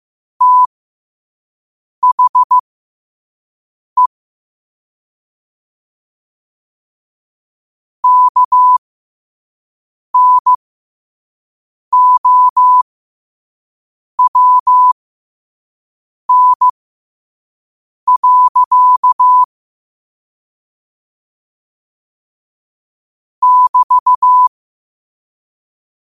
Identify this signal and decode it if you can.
Morse code, heard as a keyed tone, about 15 wpm, spaced as for 5 wpm, 1000 Hz. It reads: THE KNOWN. <BT>
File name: qotdfiles / 5wpm